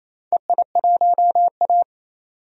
Morse code, heard as a keyed tone, about 28 wpm, 700 Hz. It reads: EI1A
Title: DXCC-100 Call Signs - 4 Characters - 28wpm